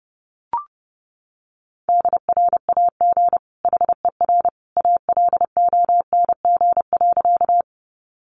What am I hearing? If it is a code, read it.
Morse code, heard as a keyed tone, about 30 wpm, 700 Hz. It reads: DRAG HER ALONG.
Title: Sentences from Top 1500 Words 30wpm